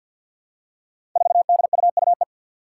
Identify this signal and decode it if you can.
Morse code, heard as a keyed tone, about 52 wpm, 700 Hz. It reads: 4DUUE
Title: Five Challenging Letter-Number Combinations 50wpm